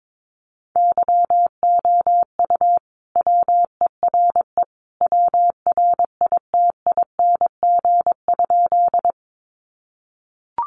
Morse code, heard as a keyed tone, about 22 wpm, 700 Hz. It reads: YOU WERE WRITING?
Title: Sentences from Top 800 Words 22wpm